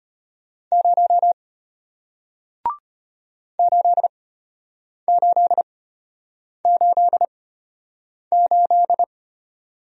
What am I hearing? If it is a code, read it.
Morse code, heard as a keyed tone, about 34 wpm, spaced as for 8 wpm, 700 Hz. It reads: 0 8888